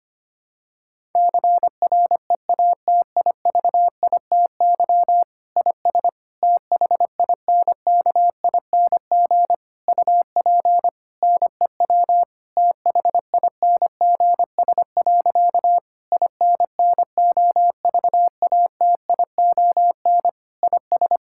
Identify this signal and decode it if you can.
Morse code, heard as a keyed tone, about 25 wpm, 700 Hz. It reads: CREATIVITY IS THINKING UP NEW THINGS. INNOVATION IS
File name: practice / 25wpm / Difficult Sentences with Hints - 25wpm